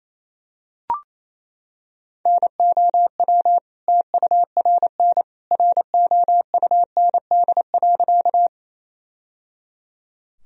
Morse code, heard as a keyed tone, about 28 wpm, 700 Hz. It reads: NOW TURN ROUND.
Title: Sentences from Top 1000 Words - Review Part 1 - 28wpm